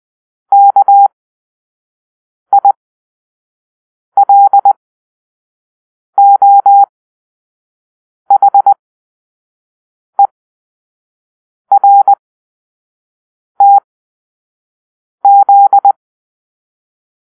Morse code, heard as a keyed tone, about 20 wpm, spaced as for 5 wpm, 800 Hz. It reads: KILOHERTZ
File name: arrl / Grp-6 Drill-4